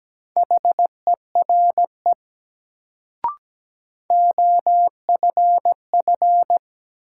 Morse code, heard as a keyed tone, about 17 wpm, 700 Hz. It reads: HERE OFF
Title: Sentences from Top 200 Words - New Word Drill - 17wpm